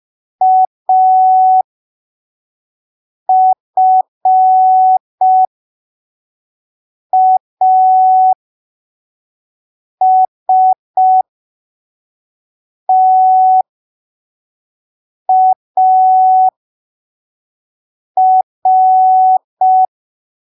Morse code, heard as a keyed tone, about 5 wpm, spaced as for 3 wpm, 750 Hz. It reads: AFASTAR
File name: arquivos / texto / texto 21 05ppm